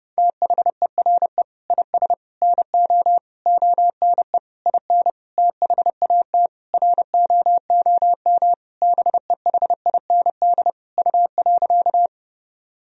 Morse code, heard as a keyed tone, about 30 wpm, 700 Hz. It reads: THERE IS NO ONE IN THAT ROOM BEHIND U.
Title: Sentences from Top 500 Words and Abbreviations 30wpm